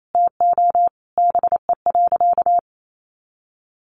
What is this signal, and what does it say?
Morse code, heard as a keyed tone, about 28 wpm, 700 Hz. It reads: TO BE.
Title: Sentences from Top 1100 Words 28wpm